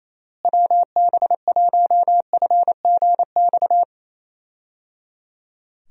Morse code, heard as a keyed tone, about 27 wpm, 700 Hz. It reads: WB1FGX
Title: USA Call Signs - 6 Characters - Repeated 3x - 28wpm